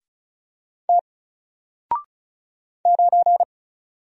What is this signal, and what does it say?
Morse code, heard as a keyed tone, about 36 wpm, 700 Hz. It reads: T 9
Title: Single Letter-Number Alt 35wpm